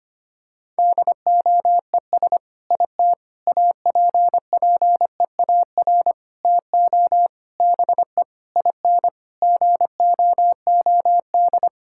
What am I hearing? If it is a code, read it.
Morse code, heard as a keyed tone, about 25 wpm, 700 Hz. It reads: DOES IT APPEAR TO BE IN GOOD
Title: Sentences from Top 1000 Words - Review Part 6 - 25wpm